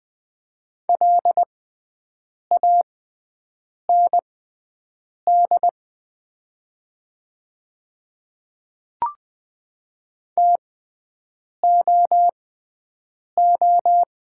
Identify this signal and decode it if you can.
Morse code, heard as a keyed tone, about 20 wpm, spaced as for 7 wpm, 700 Hz. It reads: LAND TOO
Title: 6x Character Spacing - Top 200 Words - 20wpm